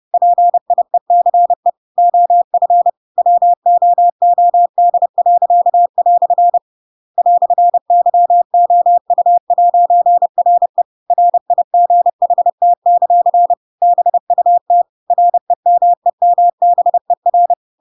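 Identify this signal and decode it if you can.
Morse code, heard as a keyed tone, about 30 wpm, 700 Hz. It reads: PIECE OF WOOD." "YOU'RE RIGHT; BUT REMEMBER
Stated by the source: Chapter 2 of The Adventures of Pinocchio read as a book in Morse code (CW) at 30 wpm, by ik1hge.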